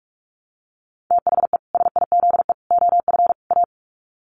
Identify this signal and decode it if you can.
Morse code, heard as a keyed tone, about 44 wpm, 700 Hz. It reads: THE SIZE OF A
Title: Sets of 4 Words 45wpm